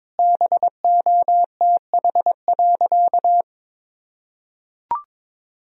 Morse code, heard as a keyed tone, about 22 wpm, 700 Hz. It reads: BOTH.
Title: Sentences from Top 300 Words and Abbreviations 22wpm